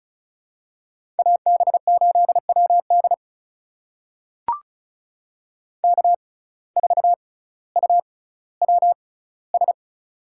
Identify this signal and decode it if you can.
Morse code, heard as a keyed tone, about 36 wpm, spaced as for 18 wpm, 700 Hz. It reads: AB8WD K4UWS
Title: USA Call Signs - 5 Characters - 6x Character Spacing - 35wpm